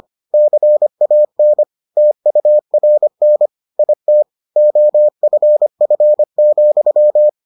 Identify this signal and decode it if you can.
Morse code, heard as a keyed tone, about 25 wpm, 600 Hz. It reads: CAN TURN IT OFF,